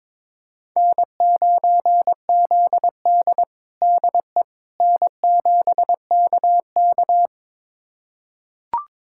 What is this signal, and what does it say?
Morse code, heard as a keyed tone, about 22 wpm, 700 Hz. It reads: N9ZD DE N7KK
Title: USA Callsign DE Callsign - 4 Characters - 22wpm